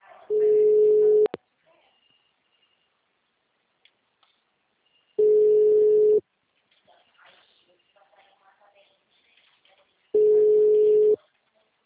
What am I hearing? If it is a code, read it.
Ringback tone (Europe, 425 Hz), with speech.
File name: out- -2041- - - .62519